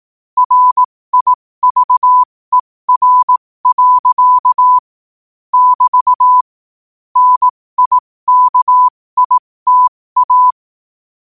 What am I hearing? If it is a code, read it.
Morse code, heard as a keyed tone, about 18 wpm, spaced as for 15 wpm, 1000 Hz. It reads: RIVER. <BT> NIKITA
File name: qotdfiles / 15wpm